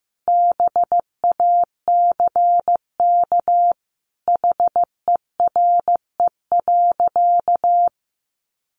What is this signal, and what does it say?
Morse code, heard as a keyed tone, about 15 wpm, 700 Hz. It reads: BACK HERE.
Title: Sentences from Top 600 Words 15wpm